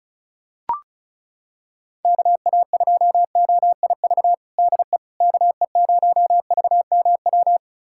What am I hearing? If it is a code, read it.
Morse code, heard as a keyed tone, about 36 wpm, 700 Hz. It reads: KA2OIV DE KE0VMW
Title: USA Callsign DE Callsign - 6 Characters - 35wpm